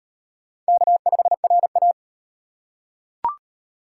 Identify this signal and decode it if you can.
Morse code, heard as a keyed tone, about 34 wpm, 700 Hz. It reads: K5RA